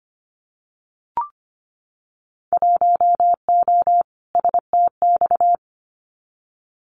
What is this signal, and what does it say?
Morse code, heard as a keyed tone, about 25 wpm, 700 Hz. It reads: 1O STX
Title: Winter Field Day Exchanges 25wpm